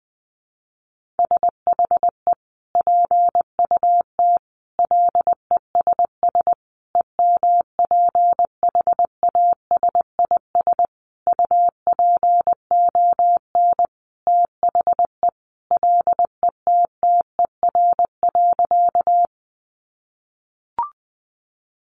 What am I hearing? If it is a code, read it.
Morse code, heard as a keyed tone, about 20 wpm, 700 Hz. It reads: SHE PUT LESS EMPHASIS UPON THE LETTER.
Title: Sentences from Top 1400 Words 20wpm